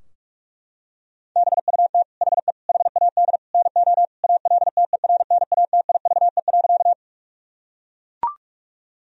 Morse code, heard as a keyed tone, about 41 wpm, 700 Hz. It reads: BUT HE HAD NO ALTERNATIVE.
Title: Sentences from Top 1000 Words - Review Part 4 - 45wpm